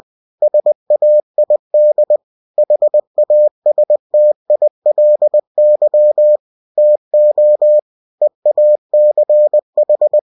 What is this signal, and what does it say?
Morse code, heard as a keyed tone, about 20 wpm, 600 Hz. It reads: SAID HASTILY TO EACH